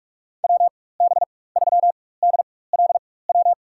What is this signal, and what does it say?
Morse code, heard as a keyed tone, about 44 wpm, spaced as for 21 wpm, 700 Hz. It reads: WB3DLW
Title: USA Call Signs - 6 Characters - 4x Character Spacing - 45wpm